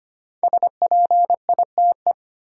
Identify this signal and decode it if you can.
Morse code, heard as a keyed tone, about 25 wpm, 700 Hz. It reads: SPITE
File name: 5 Letter Words - First 3 Letters Given - 25wpm